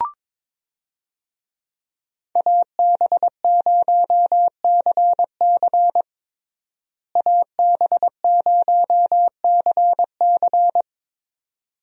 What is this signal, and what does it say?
Morse code, heard as a keyed tone, about 22 wpm, 700 Hz. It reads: AB0CC AB0CC
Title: USA Call Signs - 5 Characters - Repeated 2x - 22wpm